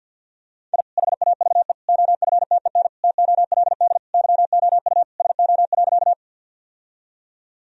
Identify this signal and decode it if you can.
Morse code, heard as a keyed tone, about 50 wpm, 700 Hz. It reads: I HAVE OFTEN TOLD YOU SO.
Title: Sentences from Top 1000 Words - Review Part 1 - 50wpm